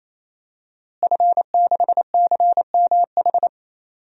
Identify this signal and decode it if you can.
Morse code, heard as a keyed tone, about 28 wpm, 700 Hz. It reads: F6CMH
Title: DXCC-100 Call Signs - 5 Characters - Repeated 3x - 28wpm